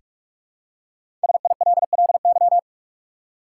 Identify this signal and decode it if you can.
Morse code, heard as a keyed tone, about 46 wpm, 700 Hz. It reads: SILLY
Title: Five Letter Words - 45wpm